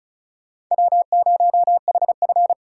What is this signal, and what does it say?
Morse code, heard as a keyed tone, about 46 wpm, 700 Hz. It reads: W0HF